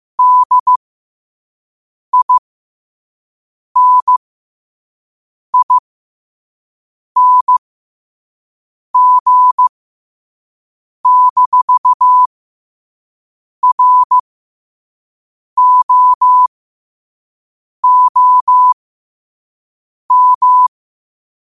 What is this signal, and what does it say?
Morse code, heard as a keyed tone, about 15 wpm, spaced as for 5 wpm, 1000 Hz. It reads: DINING-ROOM
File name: 5wpm